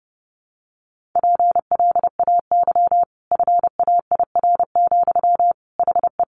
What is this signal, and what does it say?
Morse code, heard as a keyed tone, about 28 wpm, 700 Hz. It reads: PLAY FAIR, HE
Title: Sentences from Top 1300 Words 30wpm